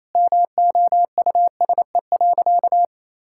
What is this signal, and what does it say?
Morse code, heard as a keyed tone, about 28 wpm, 700 Hz. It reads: MOUSE.
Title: Sentences from Top 1000 Words - Review Part 5 - 28wpm